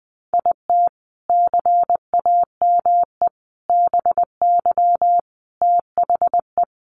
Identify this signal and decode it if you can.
Morse code, heard as a keyed tone, about 20 wpm, 700 Hz. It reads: IT CAME BY THE
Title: Sentences from Top 1000 Words 20wpm